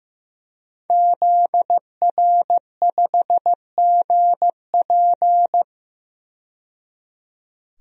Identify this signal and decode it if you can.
Morse code, heard as a keyed tone, about 15 wpm, 700 Hz. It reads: ZR5GP
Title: DXCC-100 Call Signs - 5 Characters - 15wpm